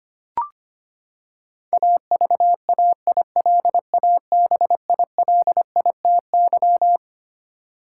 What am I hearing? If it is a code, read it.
Morse code, heard as a keyed tone, about 25 wpm, 700 Hz. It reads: AVAILABILITY
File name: Twelve Letter Words - 25wpm